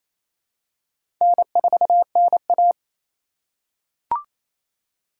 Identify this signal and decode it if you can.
Morse code, heard as a keyed tone, about 28 wpm, 700 Hz. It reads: N4NA
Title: USA Call Signs - 2x Character Spacing - 28wpm